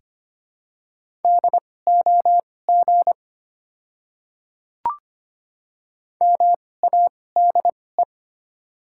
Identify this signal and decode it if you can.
Morse code, heard as a keyed tone, about 25 wpm, spaced as for 18 wpm, 700 Hz. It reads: DOG MADE
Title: 2x Character Spacing - Top 500 Words - 25wpm